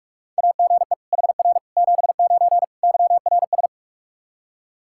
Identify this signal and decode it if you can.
Morse code, heard as a keyed tone, about 46 wpm, 700 Hz. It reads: AGE HR 79 YRS
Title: Practice QSO - Exchange 3 - More Info - 45wpm